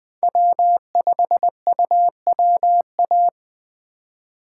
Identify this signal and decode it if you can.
Morse code, heard as a keyed tone, about 20 wpm, 700 Hz. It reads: W5UWA
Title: USA Call Signs - 5 Characters - 4x Character Spacing - 20wpm